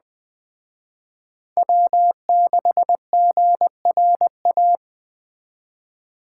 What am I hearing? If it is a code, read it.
Morse code, heard as a keyed tone, about 21 wpm, 700 Hz. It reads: W6GRA
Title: Speed Racing - USA 1x3 Call Signs 17wpm